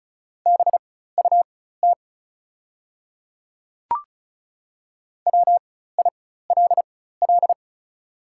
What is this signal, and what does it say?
Morse code, heard as a keyed tone, about 36 wpm, spaced as for 16 wpm, 690 Hz. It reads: BUT WILL
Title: 4x Character Spacing - Top 100 Words - 35wpm